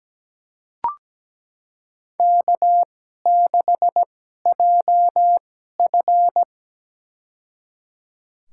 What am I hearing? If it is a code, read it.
Morse code, heard as a keyed tone, about 17 wpm, spaced as for 12 wpm, 700 Hz. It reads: K6JF
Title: USA Call Signs - 4 Characters - 2x Character Spacing - 17wpm